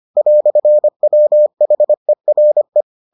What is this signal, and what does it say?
Morse code, heard as a keyed tone, about 25 wpm, 600 Hz. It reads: "WHERE